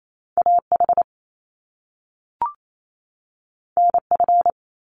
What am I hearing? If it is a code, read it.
Morse code, heard as a keyed tone, about 28 wpm, 700 Hz. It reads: AH NF
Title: Top Two Letter Combinations in Words 28wpm